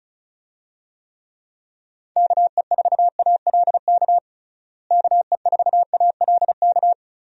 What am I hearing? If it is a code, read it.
Morse code, heard as a keyed tone, about 36 wpm, 700 Hz. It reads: KE4ALK KE4ALK
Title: USA Call Signs - 6 Characters - Repeated 3x - 35wpm